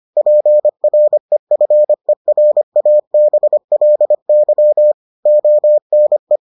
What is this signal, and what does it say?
Morse code, heard as a keyed tone, about 25 wpm, 600 Hz. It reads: PREFERABLY ONE